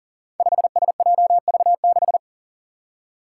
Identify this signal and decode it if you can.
Morse code, heard as a keyed tone, about 40 wpm, 700 Hz. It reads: 5SJV6
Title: Five Challenging Letter-Number Combinations 40wpm